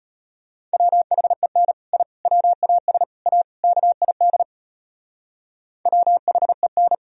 Morse code, heard as a keyed tone, about 31 wpm, 700 Hz. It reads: WHEN I WAS A KID WHEN
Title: Speed-Racing - Sets of 5 Words 25wpm